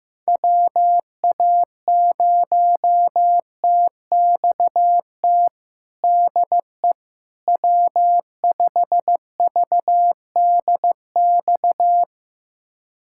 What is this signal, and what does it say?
Morse code, heard as a keyed tone, about 15 wpm, 700 Hz. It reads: WA0TXT DE W5VDX
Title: USA Callsign DE Callsign 15wpm